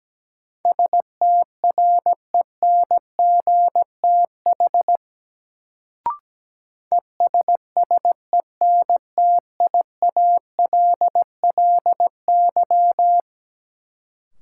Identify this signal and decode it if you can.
Morse code, heard as a keyed tone, about 17 wpm, 700 Hz. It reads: STRENGTH ESSENTIALLY
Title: Sentences from Top 1300 Words - New Word Drill Alt - 17wpm